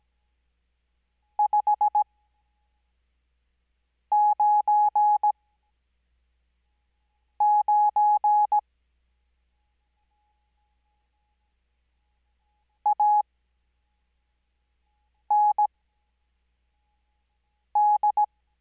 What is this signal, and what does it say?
Morse code, heard as a keyed tone, about 17 wpm, spaced as for 4 wpm, 840 Hz. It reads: 599 AND
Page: Farnsworth Spacing: 15WPM Character speed Spaced at 5WPM.